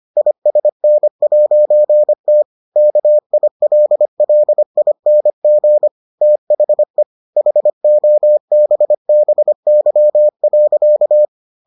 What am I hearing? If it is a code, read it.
Morse code, heard as a keyed tone, about 25 wpm, 600 Hz. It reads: ISN'T KILLING THE HOBBY.